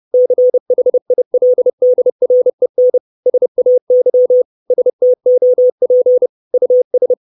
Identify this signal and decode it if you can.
Morse code, heard as a keyed tone, about 30 wpm, 500 Hz. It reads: CHILDREN SAY STOP US
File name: 632 Words 30WPM